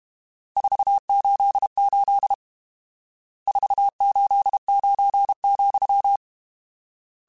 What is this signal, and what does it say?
Morse code, heard as a keyed tone, about 32 wpm, 800 Hz. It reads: 488 489,
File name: RSTs / RST-mp3s / Various RST @30